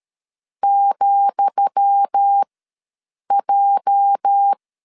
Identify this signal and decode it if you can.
Morse code, heard as a keyed tone, about 13 wpm, spaced as for 7 wpm, 800 Hz. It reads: ,J
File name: LEZIONI / 10 SIGNS / 10D COMA ws 12wpm 3618